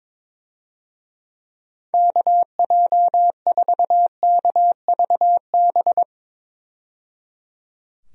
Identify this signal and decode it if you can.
Morse code, heard as a keyed tone, about 22 wpm, 700 Hz. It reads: KJ4KVB
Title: USA Call Signs - 6 Characters - 22wpm